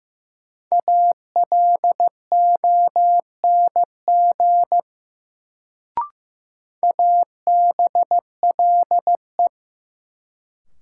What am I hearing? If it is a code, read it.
Morse code, heard as a keyed tone, about 15 wpm, 700 Hz. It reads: ALONG ABLE
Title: Sentences from Top 700 Words - New Word Drill - 15wpm